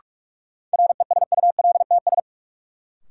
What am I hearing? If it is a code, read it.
Morse code, heard as a keyed tone, about 46 wpm, 700 Hz. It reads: RESULTS
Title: Sets of 4 Words - New Word Drill - 45wpm